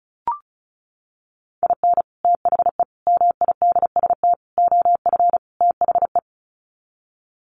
Morse code, heard as a keyed tone, about 34 wpm, 700 Hz. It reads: IN THE MIDST OF THE
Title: Sets of 5 Words 35wpm